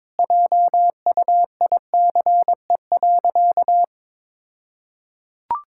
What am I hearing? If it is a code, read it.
Morse code, heard as a keyed tone, about 22 wpm, 690 Hz. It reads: JUICE.
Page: morse-code-ninja-practice-22wpm